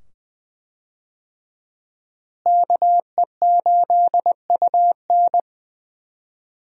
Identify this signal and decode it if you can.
Morse code, heard as a keyed tone, about 20 wpm, 700 Hz. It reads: KE8UN
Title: USA Call Signs - 5 Characters - Repeated 2x - 20wpm